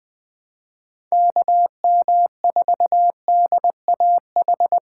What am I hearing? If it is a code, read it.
Morse code, heard as a keyed tone, about 21 wpm, 700 Hz. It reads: KM4DAH
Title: Speed Racing - USA 2x3 Call Signs 17wpm